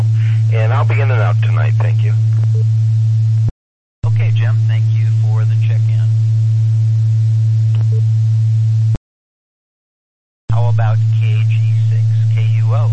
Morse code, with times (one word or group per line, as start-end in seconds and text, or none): none